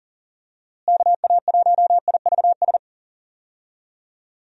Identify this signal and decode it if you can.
Morse code, heard as a keyed tone, about 40 wpm, 700 Hz. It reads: KA1IVS
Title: USA Call Signs - 6 Characters - Repeated 2x - 40wpm